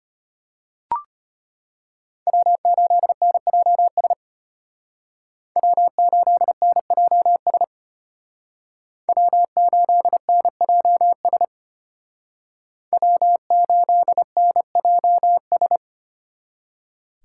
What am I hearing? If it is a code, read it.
Morse code, heard as a keyed tone, about 31 wpm, 700 Hz. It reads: W8NJS W8NJS W8NJS W8NJS
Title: Speed Racing - USA 1x3 Call Signs 25wpm